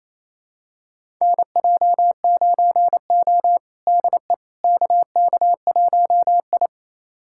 Morse code, heard as a keyed tone, about 28 wpm, 700 Hz. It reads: NJ9O DE KK1I
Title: USA Callsign DE Callsign - 4 Characters - 28wpm